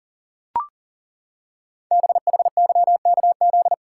Morse code, heard as a keyed tone, about 39 wpm, 700 Hz. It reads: BHYKZ
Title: Code Groups - 5 Letters - 40wpm